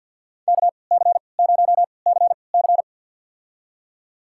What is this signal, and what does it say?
Morse code, heard as a keyed tone, about 53 wpm, spaced as for 29 wpm, 700 Hz. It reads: KC0CC
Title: USA Call Signs - 5 Characters - 3x Character Spacing - 50wpm